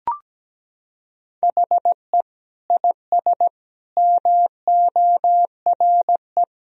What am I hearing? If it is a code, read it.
Morse code, heard as a keyed tone, about 17 wpm, 700 Hz. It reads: HE IS MORE
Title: Sentences from Top 1000 Words - Review Part 3 - 17wpm